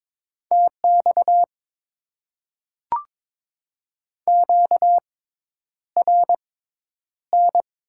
Morse code, heard as a keyed tone, about 21 wpm, spaced as for 10 wpm, 700 Hz. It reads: TX QRN
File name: Top 90 QSO Elements - 6x Character Spacing - 22wpm